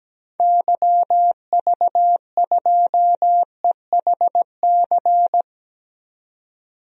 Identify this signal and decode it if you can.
Morse code, heard as a keyed tone, about 17 wpm, 700 Hz. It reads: YV2EHC
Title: DXCC-100 Call Signs - 6 Characters - 17wpm